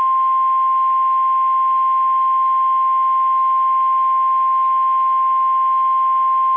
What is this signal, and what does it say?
Emergency Alert System (SAME) tones (the 1050 Hz attention tone). It is heard from 0.2 to 6.3 s.
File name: eas / KSDZKDJL / KSDZKDJL